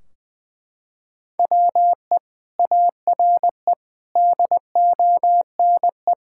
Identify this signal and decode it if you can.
Morse code, heard as a keyed tone, about 20 wpm, 700 Hz. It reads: WE ARE DONE
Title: Sentences from Top 1600 Words 20wpm